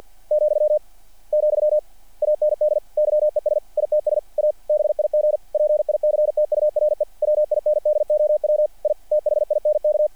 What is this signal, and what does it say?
Morse code, heard as a keyed tone, about 48 wpm, 600 Hz. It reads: , , AND YES ITS A BIG PICTURE WINDOW I THINK
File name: archived / CRSnet